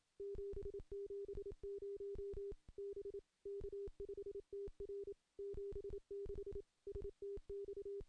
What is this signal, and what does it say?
Morse code, heard as a keyed tone, about 27 wpm, 400 Hz. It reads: 770 B K5TR 76 STX
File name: CW-SS-k5tr